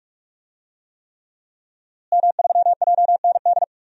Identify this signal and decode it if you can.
Morse code, heard as a keyed tone, about 46 wpm, 700 Hz. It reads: M3JND
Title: DXCC-100 Call Signs - 5 Characters - Repeated 3x - 45wpm